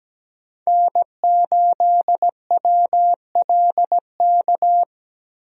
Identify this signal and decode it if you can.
Morse code, heard as a keyed tone, about 17 wpm, 700 Hz. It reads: N8WLK
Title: USA Call Signs - 5 Characters - Repeated 3x - 17wpm